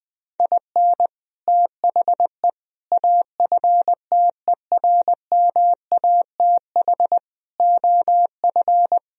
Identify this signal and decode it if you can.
Morse code, heard as a keyed tone, about 20 wpm, 700 Hz. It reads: IN THE AFTERMATH OF
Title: Speed-Racing - Sets of 5 Words 20wpm